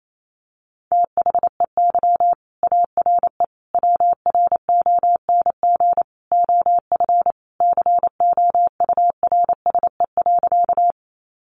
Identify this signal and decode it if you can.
Morse code, heard as a keyed tone, about 28 wpm, 700 Hz. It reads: THEY ARE WRONG OF COURSE.
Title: Sentences from Top 900 Words 28wpm